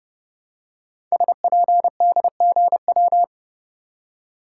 Morse code, heard as a keyed tone, about 30 wpm, 700 Hz. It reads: SPDGW
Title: Code Groups - 5 Letters - 30wpm